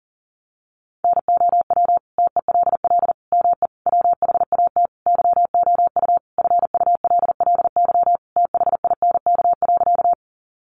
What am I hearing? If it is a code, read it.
Morse code, heard as a keyed tone, about 40 wpm, 700 Hz. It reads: NOW TELL ME WHAT YOU FULLY THINK.
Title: Sentences from Top 1300 Words 40wpm